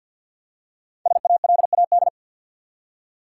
Morse code, heard as a keyed tone, about 50 wpm, 700 Hz. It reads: SALAD
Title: Five Letter Words - 50wpm